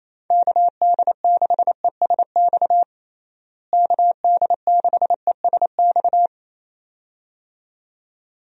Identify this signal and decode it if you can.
Morse code, heard as a keyed tone, about 28 wpm, 700 Hz. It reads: KD6ESX KD6ESX
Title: USA Call Signs - 6 Characters - Repeated 2x - 28wpm